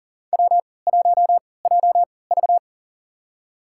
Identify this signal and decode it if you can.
Morse code, heard as a keyed tone, about 40 wpm, spaced as for 23 wpm, 700 Hz. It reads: W1JV
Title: USA Call Signs - 4 Characters - 3x Character Spacing - 40wpm